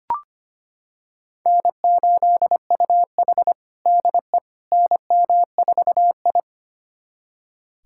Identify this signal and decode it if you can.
Morse code, heard as a keyed tone, about 25 wpm, 700 Hz. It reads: N8UH DE NM4I
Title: USA Callsign DE Callsign - 4 Characters - 25wpm